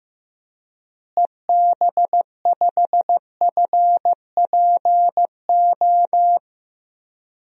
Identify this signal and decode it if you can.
Morse code, heard as a keyed tone, about 15 wpm, 700 Hz. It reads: EB5FPO
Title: DXCC-100 Call Signs - 6 Characters - Repeated 2x - 15wpm